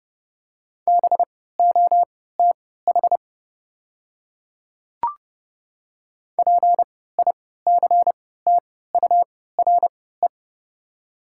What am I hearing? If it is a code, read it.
Morse code, heard as a keyed tone, about 31 wpm, spaced as for 17 wpm, 700 Hz. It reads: BOTH PICTURE
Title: 3x Character Spacing - Top 300 Words - 30wpm